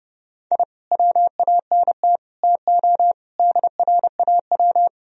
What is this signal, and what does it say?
Morse code, heard as a keyed tone, about 32 wpm, 700 Hz. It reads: I WANT TO DRAW
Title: Sentences from Top 1000 Words - Review Part 1 - 30wpm